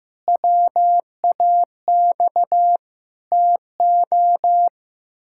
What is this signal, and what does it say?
Morse code, heard as a keyed tone, about 15 wpm, 700 Hz. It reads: WAX TO
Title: Pangram Challenge v2 - 15wpm